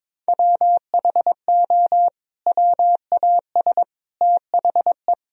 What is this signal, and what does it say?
Morse code, heard as a keyed tone, about 22 wpm, 700 Hz. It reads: WHO WAS THE
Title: Sentences from Top 300 Words and Abbreviations 22wpm